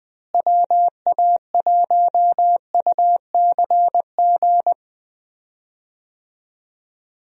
Morse code, heard as a keyed tone, about 20 wpm, 700 Hz. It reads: WA1UCG
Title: USA Call Signs - 6 Characters - Repeated 2x - 20wpm